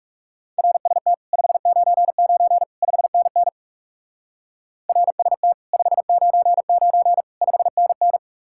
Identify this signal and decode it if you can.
Morse code, heard as a keyed tone, about 38 wpm, 700 Hz. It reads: RST 599 5NN RST 599 5NN